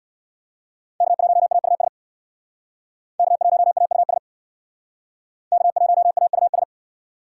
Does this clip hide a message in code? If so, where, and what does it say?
Yes: Morse code from 1.0 to 6.6 s, heard as a keyed tone, about 64 wpm, 700 Hz. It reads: K1AUS K1AUS K1AUS